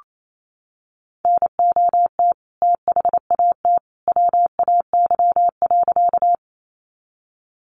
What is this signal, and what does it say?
Morse code, heard as a keyed tone, about 28 wpm, 700 Hz. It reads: NOT THAT WAY.
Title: A Sound Of Thunder 28wpm